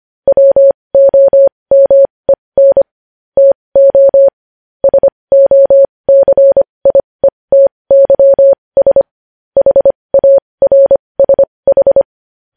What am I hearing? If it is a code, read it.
Morse code, heard as a keyed tone, about 25 wpm, spaced as for 20 wpm, 550 Hz. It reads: WOMEN TO SOCIETYS HARSH
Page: news-headlines-in-morse-code-at-20-wpm